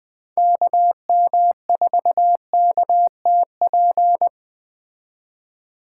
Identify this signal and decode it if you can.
Morse code, heard as a keyed tone, about 21 wpm, 700 Hz. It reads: KM4KTP